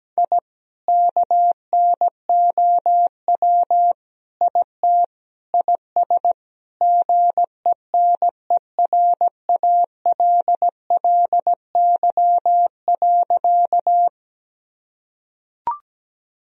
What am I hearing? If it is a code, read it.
Morse code, heard as a keyed tone, about 17 wpm, 700 Hz. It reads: I KNOW IT IS GENERALLY.
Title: Sentences from Top 1000 Words - Review Part 4 - 17wpm